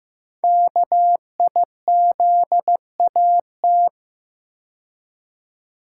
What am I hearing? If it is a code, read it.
Morse code, heard as a keyed tone, about 15 wpm, 700 Hz. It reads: KIZAT
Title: Code Groups - 5 Letters - 15wpm